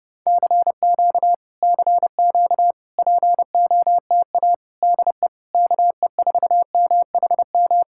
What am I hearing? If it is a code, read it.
Morse code, heard as a keyed tone, about 28 wpm, 700 Hz. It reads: CQ CQ POTA DE KE4MHM